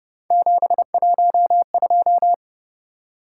Morse code, heard as a keyed tone, about 31 wpm, 700 Hz. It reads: 712